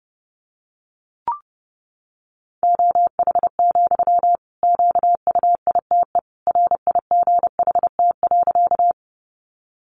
Morse code, heard as a keyed tone, about 30 wpm, 700 Hz. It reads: OH, QUITE RIGHT.
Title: Sentences from Top 700 Words 30wpm